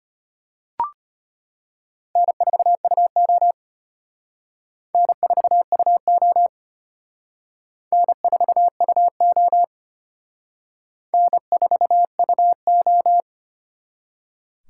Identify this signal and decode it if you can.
Morse code, heard as a keyed tone, about 32 wpm, 700 Hz. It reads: N4UO N4UO N4UO N4UO